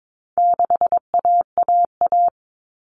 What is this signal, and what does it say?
Morse code, heard as a keyed tone, about 22 wpm, 700 Hz. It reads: 6AAA
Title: Common Cut Numbers - 4-Digit 22wpm